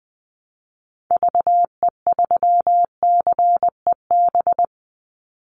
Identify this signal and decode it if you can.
Morse code, heard as a keyed tone, about 20 wpm, 700 Hz. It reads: VE3CEB
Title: Call Signs - 20wpm